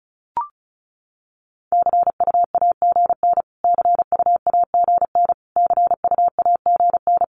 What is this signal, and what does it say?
Morse code, heard as a keyed tone, about 34 wpm, 700 Hz. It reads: CUAGN CUAGN CUAGN
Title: Abbreviations Repeated 3x - 35wpm